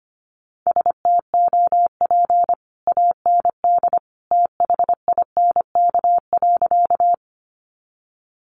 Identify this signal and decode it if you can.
Morse code, heard as a keyed tone, about 25 wpm, 700 Hz. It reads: STOP AND THINK.
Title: Sentences from Top 300 Words 25wpm